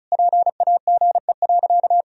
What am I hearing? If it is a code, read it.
Morse code, heard as a keyed tone, about 34 wpm, 700 Hz. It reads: PAGE.
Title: Sentences from Top 1000 Words - Review Part 1 - 35wpm